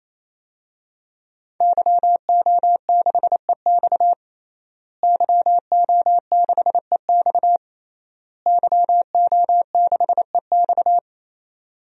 Morse code, heard as a keyed tone, about 28 wpm, 700 Hz. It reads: YO6EX YO6EX YO6EX
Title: DXCC-100 Call Signs - 5 Characters - Repeated 3x - 28wpm